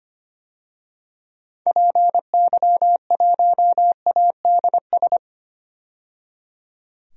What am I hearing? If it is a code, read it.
Morse code, heard as a keyed tone, about 25 wpm, 700 Hz. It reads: PY1ADS
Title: DXCC-100 Call Signs - 6 Characters - 25wpm